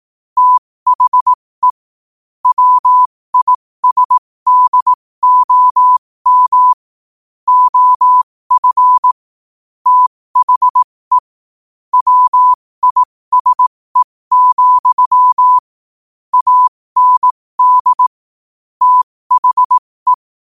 Morse code, heard as a keyed tone, about 18 wpm, spaced as for 15 wpm, 1000 Hz. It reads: THE WISDOM OF THE WISE, AND THE
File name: qotdfiles / 15wpm